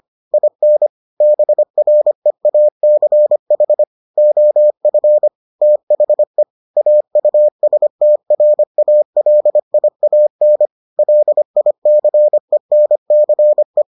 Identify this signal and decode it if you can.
Morse code, heard as a keyed tone, about 25 wpm, 600 Hz. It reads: IN BREACH OF THE AUSTRALIAN LICENCE